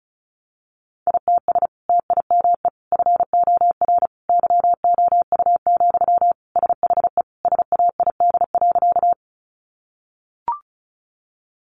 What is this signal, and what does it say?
Morse code, heard as a keyed tone, about 34 wpm, 700 Hz. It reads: ITS TIME FOR YOU, SHE SAID.
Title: Sentences from Top 600 Words 35wpm